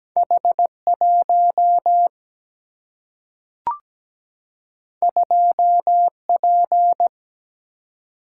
Morse code, heard as a keyed tone, about 17 wpm, 700 Hz. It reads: H1 2P
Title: Single-Letter Single-Number 17wpm